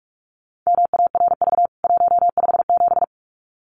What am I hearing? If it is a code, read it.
Morse code, heard as a keyed tone, about 44 wpm, 700 Hz. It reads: MARV 157
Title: CWT Exchange 45wpm